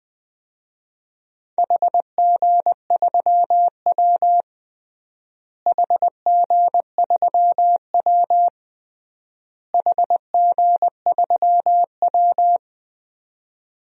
Morse code, heard as a keyed tone, about 20 wpm, 700 Hz. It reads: HG3W HG3W HG3W
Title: DXCC-100 Call Signs - 4 Characters - Repeated 3x - 20wpm